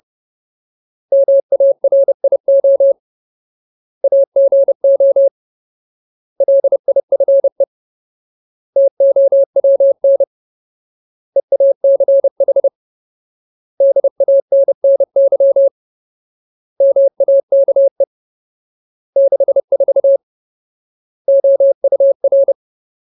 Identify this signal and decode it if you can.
Morse code, heard as a keyed tone, about 30 wpm, 550 Hz. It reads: MARIO AGO LIFE TOWN EACH DANNY MAKE 64 OUR
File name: Ultimate Word List / 30WPM Ultimate